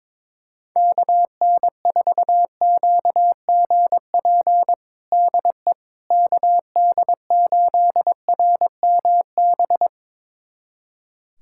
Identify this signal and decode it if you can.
Morse code, heard as a keyed tone, about 22 wpm, 700 Hz. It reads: KN4QGP DE KD8RMB
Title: USA Callsign DE Callsign 22wpm